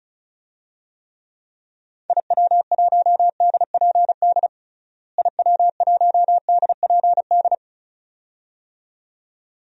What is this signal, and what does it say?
Morse code, heard as a keyed tone, about 36 wpm, 700 Hz. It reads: IW1DPD IW1DPD
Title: DXCC-100 Call Signs - 6 Characters - Repeated 2x - 35wpm